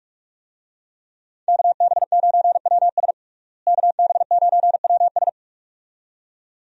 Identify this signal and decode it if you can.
Morse code, heard as a keyed tone, about 45 wpm, 700 Hz. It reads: KB9WS KB9WS
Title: USA Call Signs - 5 Characters - Repeated 2x - 45wpm